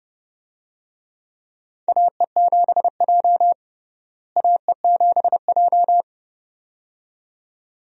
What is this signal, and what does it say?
Morse code, heard as a keyed tone, about 30 wpm, 700 Hz. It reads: AE7J AE7J
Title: USA Call Signs - 4 Characters - Repeated 2x - 30wpm